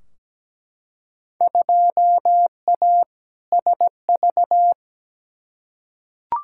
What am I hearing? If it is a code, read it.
Morse code, heard as a keyed tone, about 17 wpm, 700 Hz. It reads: 2A SV
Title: Field Day Exchanges - 17wpm